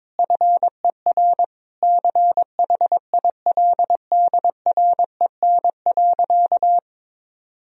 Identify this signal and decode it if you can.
Morse code, heard as a keyed tone, about 22 wpm, 700 Hz. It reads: FER CHILDREN.
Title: Sentences from Top 400 Words and Abbreviations 22wpm